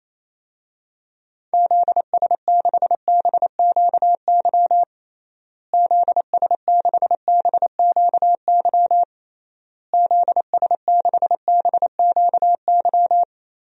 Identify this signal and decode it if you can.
Morse code, heard as a keyed tone, about 28 wpm, 700 Hz. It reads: ZS6BQY ZS6BQY ZS6BQY
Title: DXCC-100 Call Signs - 6 Characters - Repeated 3x - 28wpm